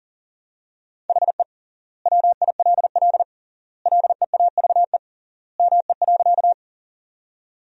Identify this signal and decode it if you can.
Morse code, heard as a keyed tone, about 40 wpm, 700 Hz. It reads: HE WILL LEAVE ME.
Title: 3x Word-Spacing - Sentences from Top 400 Words - 40wpm